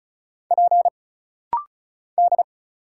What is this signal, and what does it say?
Morse code, heard as a keyed tone, about 34 wpm, 700 Hz. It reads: P D
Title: Single Letters - Rapid-Fire 35wpm